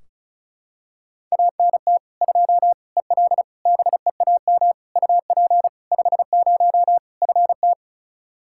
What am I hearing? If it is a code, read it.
Morse code, heard as a keyed tone, about 34 wpm, 700 Hz. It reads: ANT 2 EL BEAM UP 50 FT
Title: Practice QSO - Exchange 2 - Info - 35wpm